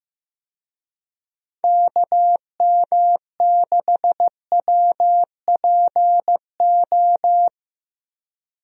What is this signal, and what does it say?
Morse code, heard as a keyed tone, about 15 wpm, 700 Hz. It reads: KM6WPO
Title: USA Call Signs - 6 Characters - Repeated 2x - 15wpm